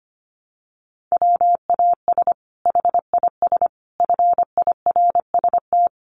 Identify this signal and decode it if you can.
Morse code, heard as a keyed tone, about 25 wpm, 700 Hz. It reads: WAS HIS FIRST
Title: Sentences from Top 800 Words 25wpm